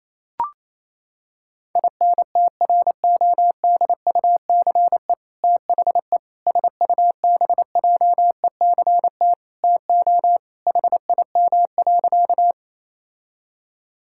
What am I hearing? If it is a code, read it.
Morse code, heard as a keyed tone, about 28 wpm, 700 Hz. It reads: INTRODUCE THE SUBJECT TO HIM.
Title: Sentences from Top 1600 Words 28wpm